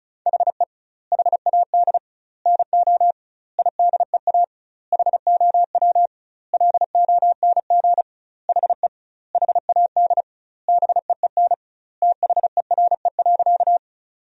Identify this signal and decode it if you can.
Morse code, heard as a keyed tone, about 34 wpm, 700 Hz. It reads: HE HAD NO IDEA HOW LONG HE HAD BEEN THERE.
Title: 2x Word-Spacing - Sentences from Top 300 Words - 35wpm